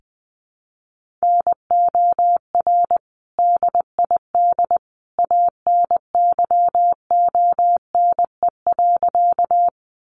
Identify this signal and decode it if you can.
Morse code, heard as a keyed tone, about 20 wpm, 700 Hz. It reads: NOR DID ANYONE.
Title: Sentences from Top 1500 Words 20wpm